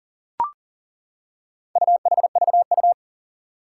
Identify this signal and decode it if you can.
Morse code, heard as a keyed tone, about 40 wpm, 700 Hz. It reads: UHVU
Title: Four Challenging Letter-Number Combinations 40wpm